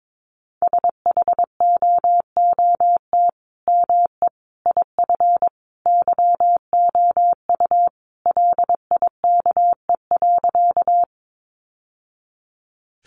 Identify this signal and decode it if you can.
Morse code, heard as a keyed tone, about 22 wpm, 700 Hz. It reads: SHOOT ME IF YOU LIKE.
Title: Sentences from Top 1200 Words 22wpm